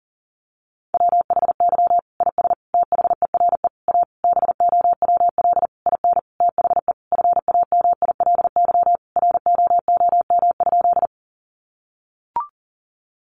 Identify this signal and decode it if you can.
Morse code, heard as a keyed tone, about 40 wpm, 700 Hz. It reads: WHY IS THERE A BOWL IN THE FAMILY ROOM?
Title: Sentences from Top 1200 Words 40wpm